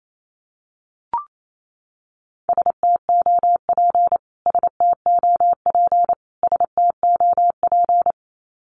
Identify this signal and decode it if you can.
Morse code, heard as a keyed tone, about 28 wpm, 700 Hz. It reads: STOP STOP STOP
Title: Top 500 Words Repeated 28wpm